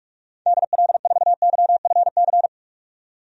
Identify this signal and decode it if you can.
Morse code, heard as a keyed tone, about 46 wpm, 700 Hz. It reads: DL4YFC
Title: DXCC-100 Call Signs - 6 Characters - 45wpm